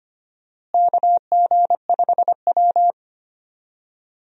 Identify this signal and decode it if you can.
Morse code, heard as a keyed tone, about 25 wpm, 700 Hz. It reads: KG5W
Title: USA Call Signs - 4 Characters - Repeated 2x - 25wpm